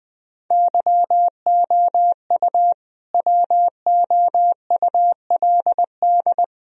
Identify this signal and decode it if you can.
Morse code, heard as a keyed tone, about 20 wpm, 700 Hz. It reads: YOU WOULD
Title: Sentences from Top 1000 Words - Review Part 1 - 20wpm